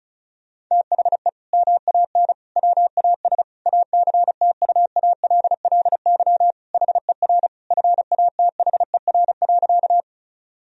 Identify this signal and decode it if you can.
Morse code, heard as a keyed tone, about 34 wpm, 700 Hz. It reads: THE MAN WAS ACTUALLY HER FATHER.
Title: Sentences from Top 1000 Words - Review Part 4 - 35wpm